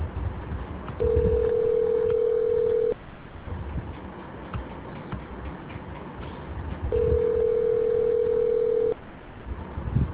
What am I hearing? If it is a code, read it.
Ringback tone (North America), 440 + 480 Hz.